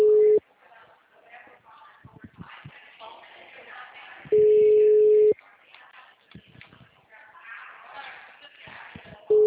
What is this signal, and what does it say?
Ringback tone (Europe, 425 Hz), with speech.